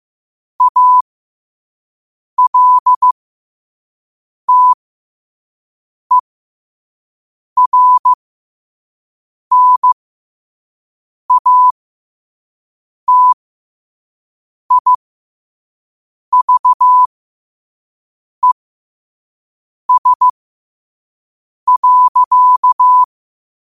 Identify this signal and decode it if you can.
Morse code, heard as a keyed tone, about 15 wpm, spaced as for 5 wpm, 1000 Hz. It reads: ALTERNATIVES.
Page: qotd-at-5-wpm